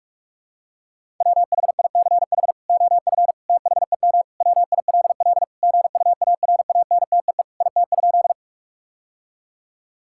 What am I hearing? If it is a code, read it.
Morse code, heard as a keyed tone, about 41 wpm, 700 Hz. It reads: WHICH OF THEM WILL GUARANTEE IT?
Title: Sentences from Top 1600 Words 45wpm